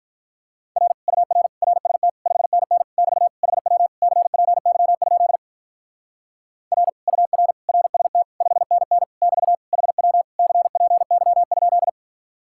Morse code, heard as a keyed tone, about 43 wpm, 700 Hz. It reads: R UR RST 5NN <BT> HW CPY? R UR RST 5NN <BT> HW CPY?
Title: Speed-Racing Common QSO Phrases - 35wpm